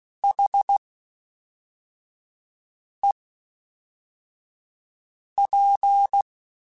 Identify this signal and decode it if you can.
Morse code, heard as a keyed tone, about 16 wpm, spaced as for 4 wpm, 800 Hz. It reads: HEP
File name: H Prefixes / H mp3s